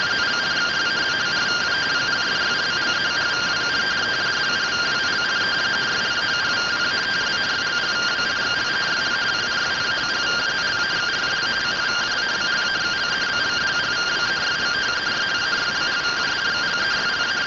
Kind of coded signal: two-tone teleprinter signal

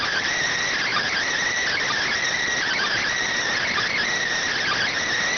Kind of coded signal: Emergency Alert System (SAME) tones

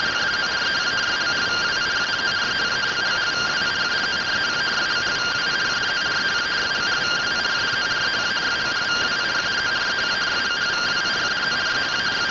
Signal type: two-tone teleprinter signal

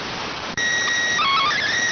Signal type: SSTV (slow-scan television)

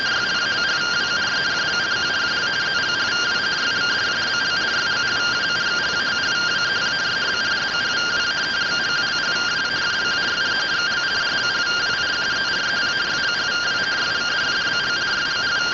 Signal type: two-tone teleprinter signal

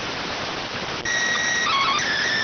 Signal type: SSTV (slow-scan television)

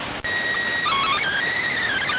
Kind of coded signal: SSTV (slow-scan television)

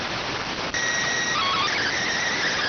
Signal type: SSTV (slow-scan television)

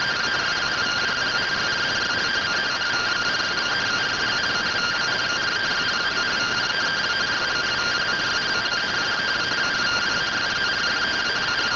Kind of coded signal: two-tone teleprinter signal

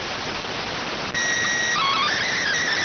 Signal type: SSTV (slow-scan television)

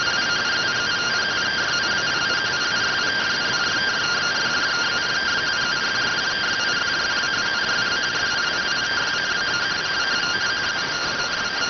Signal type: two-tone teleprinter signal

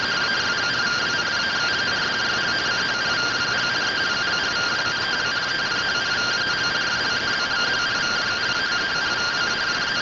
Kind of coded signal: two-tone teleprinter signal

